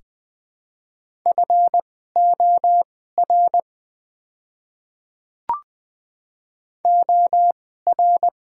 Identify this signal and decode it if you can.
Morse code, heard as a keyed tone, about 20 wpm, spaced as for 14 wpm, 700 Hz. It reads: FOR OR
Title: 2x Character Spacing - Top 500 Words - 20wpm